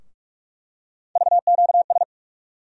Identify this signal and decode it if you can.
Morse code, heard as a keyed tone, about 42 wpm, 700 Hz. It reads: VQS
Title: USA Call Sign Postfixes - 3 Letters 45wpm